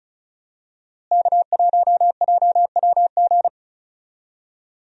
Morse code, heard as a keyed tone, about 36 wpm, 700 Hz. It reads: K1JWG
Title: USA Call Signs - 5 Characters - Repeated 3x - 35wpm